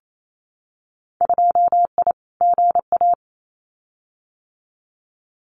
Morse code, heard as a keyed tone, about 28 wpm, 700 Hz. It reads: 2I GA
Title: Winter Field Day Exchanges 28wpm